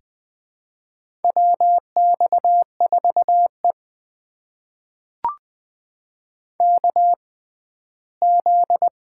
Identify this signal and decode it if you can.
Morse code, heard as a keyed tone, about 20 wpm, spaced as for 14 wpm, 700 Hz. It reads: WX4E KZ
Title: USA Call Signs - 4 Characters - 6x Character Spacing - 20wpm